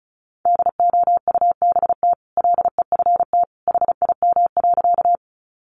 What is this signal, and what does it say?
Morse code, heard as a keyed tone, about 34 wpm, 700 Hz. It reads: DOUBT LEFT HIM.
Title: Sentences from Top 1500 Words 35wpm